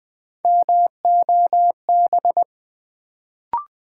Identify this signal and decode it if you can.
Morse code, heard as a keyed tone, about 20 wpm, 700 Hz. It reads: MOB
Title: Build A Word - 1 Letter At A Time - 3-Letter Words - 20wpm